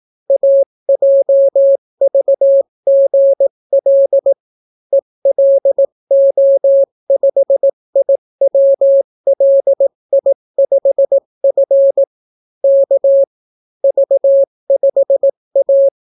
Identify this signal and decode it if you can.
Morse code, heard as a keyed tone, about 18 wpm, 550 Hz. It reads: AJVGL ELO5IWLI5F K V5A